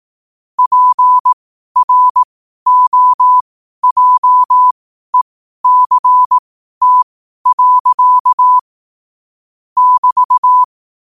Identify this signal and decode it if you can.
Morse code, heard as a keyed tone, about 18 wpm, spaced as for 12 wpm, 1000 Hz. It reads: PROJECT. <BT>
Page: Quotes for Thu, 14 Aug 2025 in Morse Code at 12 words per minute.